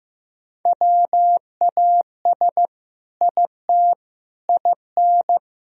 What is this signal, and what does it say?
Morse code, heard as a keyed tone, about 15 wpm, 700 Hz. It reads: WAS IT IN
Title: Sentences from Top 1000 Words - Review Part 4 - 15wpm